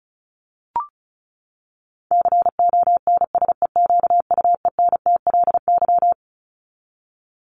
Morse code, heard as a keyed tone, about 34 wpm, 700 Hz. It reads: CONSEQUENTLY
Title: LY Suffix - Top 100 Words with Base Words 35wpm